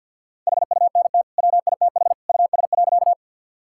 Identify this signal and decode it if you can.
Morse code, heard as a keyed tone, about 50 wpm, 700 Hz. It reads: HUNT WITH US.
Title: Sentences from Top 1600 Words 50wpm